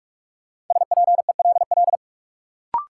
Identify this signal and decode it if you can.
Morse code, heard as a keyed tone, about 41 wpm, 700 Hz. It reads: SPELL
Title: Sentences from Top 1000 Words - Review Part 1 - New Word Drill - 45wpm